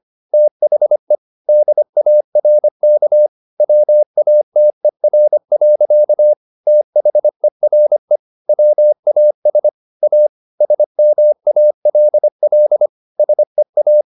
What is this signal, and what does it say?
Morse code, heard as a keyed tone, about 25 wpm, 600 Hz. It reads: THE DARK WATER. THERE WAS A SMALL SEA
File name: the old man and the sea part 1 25wpm